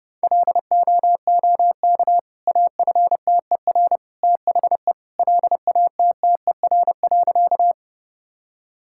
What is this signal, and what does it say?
Morse code, heard as a keyed tone, about 30 wpm, 700 Hz. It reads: LOOK AFTER THE LATTER.
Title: Sentences from Top 1600 Words 30wpm